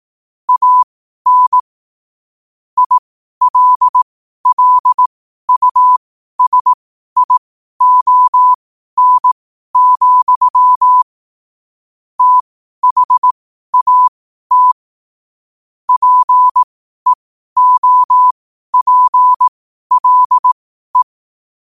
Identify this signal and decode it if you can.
Morse code, heard as a keyed tone, about 18 wpm, spaced as for 12 wpm, 1000 Hz. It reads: AN ILLUSION, THAT PEOPLE
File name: qotdfiles / 12wpm